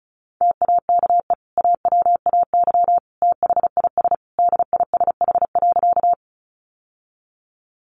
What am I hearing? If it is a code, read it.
Morse code, heard as a keyed tone, about 34 wpm, 700 Hz. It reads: TAKE AWAY THIS DISH.
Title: Sentences from Top 1300 Words 35wpm